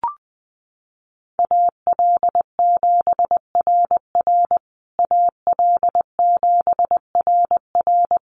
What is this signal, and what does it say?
Morse code, heard as a keyed tone, about 20 wpm, 700 Hz. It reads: AL7RR AL7RR
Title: USA Call Signs Repeated 3x 20wpm